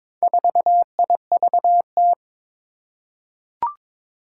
Morse code, heard as a keyed tone, about 22 wpm, 700 Hz. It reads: 4IVT
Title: Four Challenging Letter-Number Combinations 22wpm